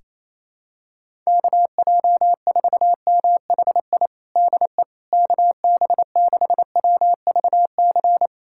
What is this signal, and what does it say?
Morse code, heard as a keyed tone, about 28 wpm, 700 Hz. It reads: KJ4MHI DE KB6WVC
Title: USA Callsign DE Callsign - 6 Characters - 28wpm